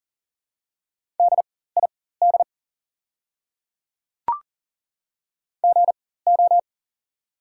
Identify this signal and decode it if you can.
Morse code, heard as a keyed tone, about 40 wpm, spaced as for 19 wpm, 700 Hz. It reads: DID GO
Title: 4x Character Spacing - Top 300 Words - 40wpm